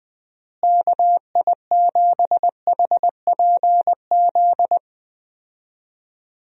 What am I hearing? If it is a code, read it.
Morse code, heard as a keyed tone, about 20 wpm, 700 Hz. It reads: KI7HPZ
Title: USA Call Signs - 6 Characters - Repeated 2x - 20wpm